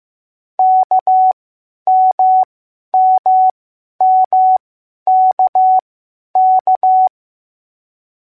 Morse code, heard as a keyed tone, about 15 wpm, spaced as for 10 wpm, 750 Hz. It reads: KMMMKK